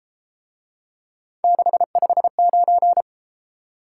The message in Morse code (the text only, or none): 659